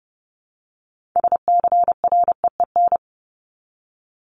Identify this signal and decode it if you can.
Morse code, heard as a keyed tone, about 30 wpm, 700 Hz. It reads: SCREEN
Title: Top 751-1000 Words 30wpm